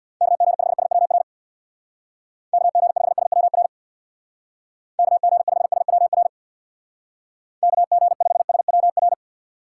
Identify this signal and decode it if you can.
Morse code, heard as a keyed tone, about 62 wpm, 700 Hz. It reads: KG5SWR KG5SWR KG5SWR KG5SWR
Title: Speed Racing - USA 2x3 Call Signs 50wpm